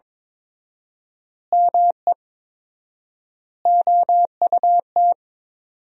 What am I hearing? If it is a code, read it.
Morse code, heard as a keyed tone, about 22 wpm, 700 Hz. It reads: ME OUT
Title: 4x Word-Spacing - Sentences from Top 400 Words - 22wpm